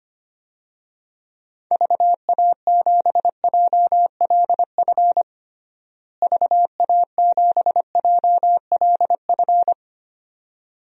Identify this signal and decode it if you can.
Morse code, heard as a keyed tone, about 25 wpm, 700 Hz. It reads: VA7JLF VA7JLF
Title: DXCC-100 Call Signs - 6 Characters - Repeated 2x - 25wpm